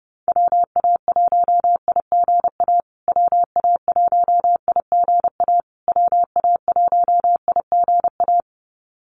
Morse code, heard as a keyed tone, about 30 wpm, 700 Hz. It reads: WA1IGA WA1IGA WA1IGA
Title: USA Call Signs Repeated 3x 30wpm